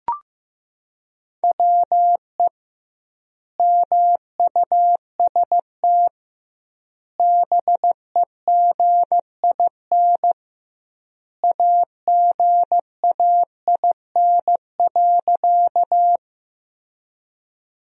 Morse code, heard as a keyed tone, about 15 wpm, 700 Hz. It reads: WE MUST BEGIN AGAIN.
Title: 2x Word-Spacing - Sentences from Top 300 Words - 15wpm